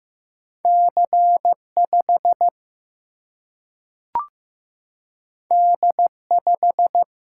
Morse code, heard as a keyed tone, about 15 wpm, 700 Hz. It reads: C 5 D 5
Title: Single-Letter Single-Number 15wpm